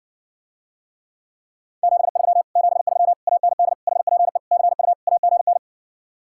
Morse code, heard as a keyed tone, about 70 wpm, 700 Hz. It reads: 73 73 AND HPE CU AGN